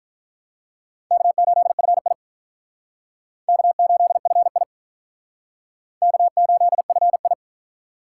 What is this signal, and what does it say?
Morse code, heard as a keyed tone, about 45 wpm, 700 Hz. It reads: K8FI K8FI K8FI